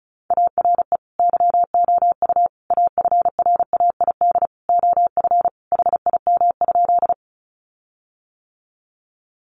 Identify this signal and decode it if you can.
Morse code, heard as a keyed tone, about 34 wpm, 700 Hz. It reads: ARE YOU AFRAID OF HIM?
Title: Sentences from Top 1500 Words 35wpm